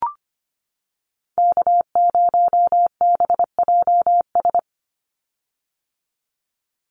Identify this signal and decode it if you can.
Morse code, heard as a keyed tone, about 25 wpm, 700 Hz. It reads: K0BJS
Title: USA Call Signs 25wpm